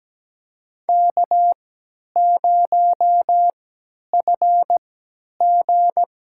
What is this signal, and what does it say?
Morse code, heard as a keyed tone, about 17 wpm, spaced as for 10 wpm, 700 Hz. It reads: K0FG
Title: USA Call Signs - 4 Characters - 3x Character Spacing - 17wpm